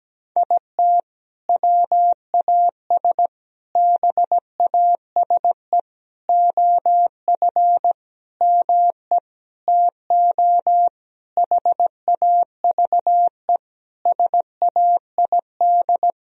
Morse code, heard as a keyed tone, about 17 wpm, 700 Hz. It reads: IT WAS BASE OF ME TO HAVE SAID
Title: Sentences from Top 1000 Words - Review Part 1 - 17wpm